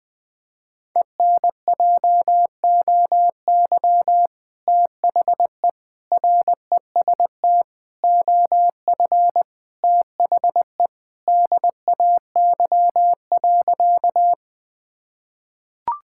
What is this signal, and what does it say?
Morse code, heard as a keyed tone, about 20 wpm, 700 Hz. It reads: ENJOY THE REST OF THE DAY.
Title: Sentences with 2-Word Phrases - 20wpm